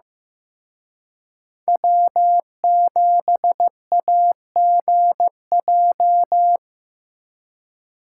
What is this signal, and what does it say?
Morse code, heard as a keyed tone, about 18 wpm, 700 Hz. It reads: W7AGJ
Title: Speed Racing - USA 1x3 Call Signs 15wpm